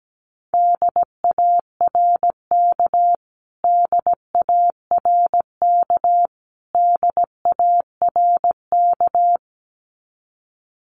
Morse code, heard as a keyed tone, about 17 wpm, 700 Hz. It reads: DARK DARK DARK
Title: Top 500 Words Repeated 17wpm